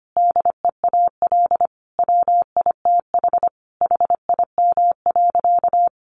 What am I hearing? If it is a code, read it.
Morse code, heard as a keyed tone, about 25 wpm, 700 Hz. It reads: DEAL WITH HIM.
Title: Sentences from Top 800 Words 25wpm